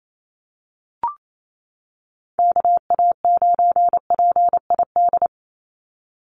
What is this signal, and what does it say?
Morse code, heard as a keyed tone, about 28 wpm, 700 Hz. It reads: KA9PID
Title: USA Call Signs 28wpm